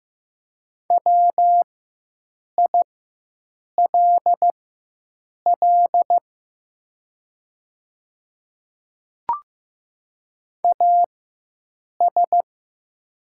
Morse code, heard as a keyed tone, about 15 wpm, spaced as for 7 wpm, 700 Hz. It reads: WILL AS
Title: 4x Character Spacing - Top 100 Words - 15wpm